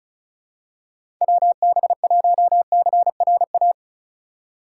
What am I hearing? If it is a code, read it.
Morse code, heard as a keyed tone, about 36 wpm, 700 Hz. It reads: WB1CRA
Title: USA Call Signs - 6 Characters - 4x Character Spacing - 35wpm